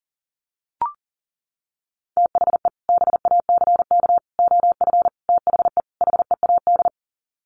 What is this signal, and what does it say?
Morse code, heard as a keyed tone, about 40 wpm, 700 Hz. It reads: THE BACK OF THE HEAD
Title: Sets of 5 Words Encore2 40wpm